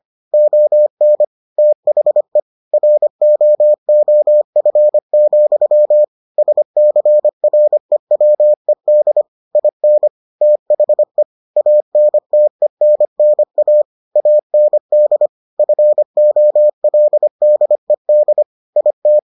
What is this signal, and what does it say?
Morse code, heard as a keyed tone, about 25 wpm, 600 Hz. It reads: ON THE ROOF, SCREWED IN THE ANTENNA AND FOLDED IT